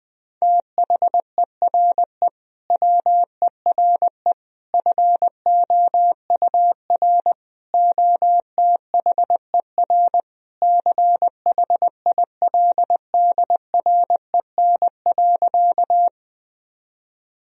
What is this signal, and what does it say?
Morse code, heard as a keyed tone, about 21 wpm, 700 Hz. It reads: THERE WERE FOUR OTHER CHILDREN.
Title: Sentences from Top 1000 Words - Review Part 1 - 20wpm